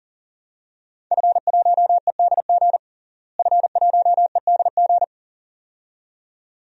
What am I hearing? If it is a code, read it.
Morse code, heard as a keyed tone, about 40 wpm, 700 Hz. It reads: F1EDG F1EDG
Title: DXCC-100 Call Signs - 5 Characters - Repeated 2x - 40wpm